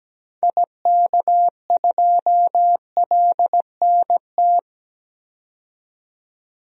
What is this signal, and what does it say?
Morse code, heard as a keyed tone, about 17 wpm, 700 Hz. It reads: IK2LNT
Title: DXCC-100 Call Signs - 6 Characters - 17wpm